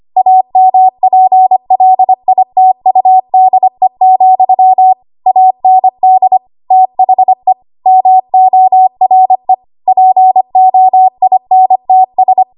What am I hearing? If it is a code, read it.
Morse code, heard as a keyed tone, about 25 wpm, 750 Hz. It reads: AMPLITUDE, AND THE MORE POINTS
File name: Archive / 25 WPM / 25WPM